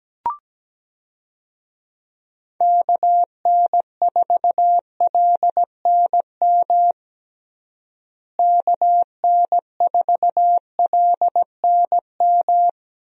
Morse code, heard as a keyed tone, about 17 wpm, 700 Hz. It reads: KN4LNM KN4LNM
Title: USA Call Signs - 6 Characters - Repeated 2x - 17wpm